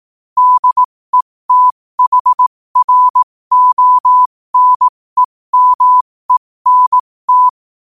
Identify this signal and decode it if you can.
Morse code, heard as a keyed tone, about 18 wpm, spaced as for 15 wpm, 1000 Hz. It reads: DETHRONEMENT
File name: qotdfiles / 15wpm